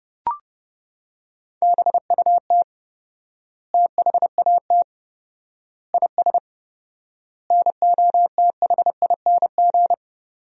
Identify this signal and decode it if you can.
Morse code, heard as a keyed tone, about 32 wpm, 700 Hz. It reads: BUT THAT IS NOTHING
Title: 4x Word-Spacing - Sentences from Top 500 Words - 30wpm